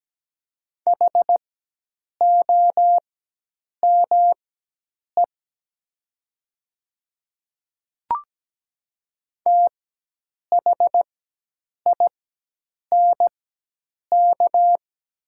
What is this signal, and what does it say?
Morse code, heard as a keyed tone, about 17 wpm, spaced as for 8 wpm, 700 Hz. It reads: HOME THINK
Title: 4x Character Spacing - Top 200 Words - 17wpm